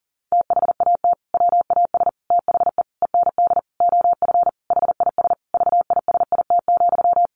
Morse code, heard as a keyed tone, about 40 wpm, 700 Hz. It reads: THAT WAS THE END OF HIS VISIT,
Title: Sentences from Top 1400 Words 40wpm